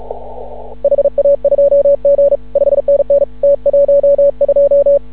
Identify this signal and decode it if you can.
Morse code, heard as a keyed tone, about 36 wpm, 560 Hz. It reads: HA2G 5NN T12
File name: CW / ha6oi 2